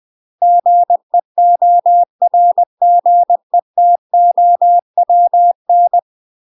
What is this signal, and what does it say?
Morse code, heard as a keyed tone, about 20 wpm, 700 Hz. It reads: GEORGETOWN
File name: UK MASTER QSOs TRAINING / 20WPM / UK MASTER QSOs 0032